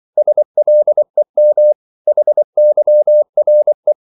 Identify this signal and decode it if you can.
Morse code, heard as a keyed tone, about 24 wpm, 600 Hz. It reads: SLEM HYRE
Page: Korte danske ord 24wpm